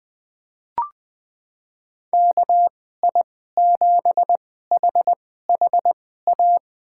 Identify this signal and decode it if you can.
Morse code, heard as a keyed tone, about 21 wpm, spaced as for 15 wpm, 700 Hz. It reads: KI7HHA
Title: USA Call Signs - 6 Characters - 2x Character Spacing - 20wpm